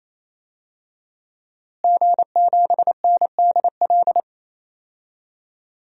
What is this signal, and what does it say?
Morse code, heard as a keyed tone, about 27 wpm, 700 Hz. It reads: G7NDL
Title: DXCC-100 Call Signs - 5 Characters - Repeated 3x - 28wpm